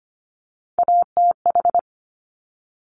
Morse code, heard as a keyed tone, about 25 wpm, 700 Hz. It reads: ATH
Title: Top Three Letter Combinations in Words 25wpm